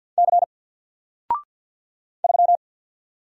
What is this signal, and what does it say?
Morse code, heard as a keyed tone, about 51 wpm, 700 Hz. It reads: C 3
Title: Single Letter-Number 50wpm